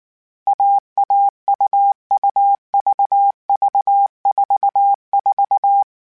Morse code, heard as a keyed tone, about 19 wpm, 800 Hz. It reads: AAUUVV44